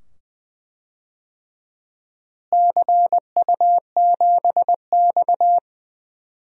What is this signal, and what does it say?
Morse code, heard as a keyed tone, about 20 wpm, 700 Hz. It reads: CU7X
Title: DXCC-100 Call Signs - 4 Characters - Repeated 2x - 20wpm